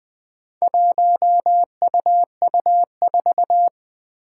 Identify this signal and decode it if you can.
Morse code, heard as a keyed tone, about 20 wpm, 700 Hz. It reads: 1UU4
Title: Four Challenging Letter-Number Combinations 20wpm